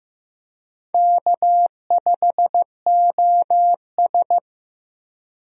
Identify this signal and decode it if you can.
Morse code, heard as a keyed tone, about 16 wpm, 700 Hz. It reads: K5OS
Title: USA Call Signs - 4 Characters - Repeated 3x - 15wpm